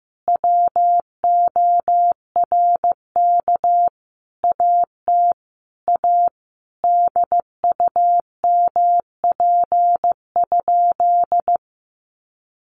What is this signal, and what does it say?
Morse code, heard as a keyed tone, about 15 wpm, 700 Hz. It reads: WORK AT A DUMP?
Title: Sentences from Top 1300 Words 15wpm